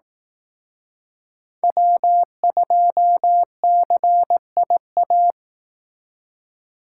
Morse code, heard as a keyed tone, about 18 wpm, 700 Hz. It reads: W2CIA
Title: Speed Racing - USA 1x3 Call Signs 15wpm